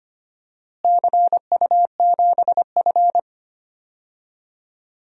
Morse code, heard as a keyed tone, about 25 wpm, 700 Hz. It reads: CU7F
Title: DXCC-100 Call Signs - 4 Characters - Repeated 2x - 25wpm